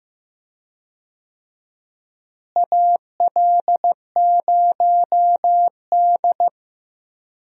Morse code, heard as a keyed tone, about 15 wpm, 700 Hz. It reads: AL0D
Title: USA Call Signs - 4 Characters - Repeated 2x - 15wpm